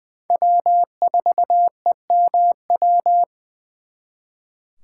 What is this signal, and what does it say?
Morse code, heard as a keyed tone, about 24 wpm, 700 Hz. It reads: W4EMW
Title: Speed Racing - USA 1x3 Call Signs 20wpm